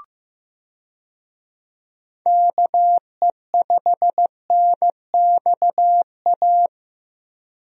Morse code, heard as a keyed tone, about 15 wpm, 700 Hz. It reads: KE5NXA
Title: USA Call Signs - 6 Characters - 15wpm